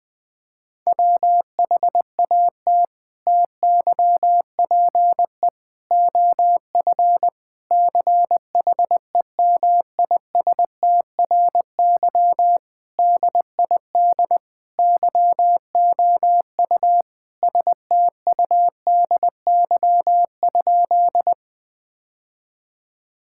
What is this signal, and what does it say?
Morse code, heard as a keyed tone, about 20 wpm, 700 Hz. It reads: WHAT TYPE OF CHEMISTRY DID YOU STUDY?
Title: Sentences from Top 1000 Words - Review Part 6 - 20wpm